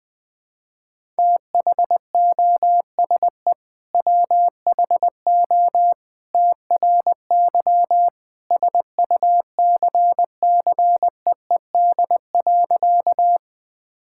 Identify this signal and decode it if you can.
Morse code, heard as a keyed tone, about 20 wpm, 700 Hz. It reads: THOSE WHO TRY SUCCEED.
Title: Sentences with 2-Word Phrases - 20wpm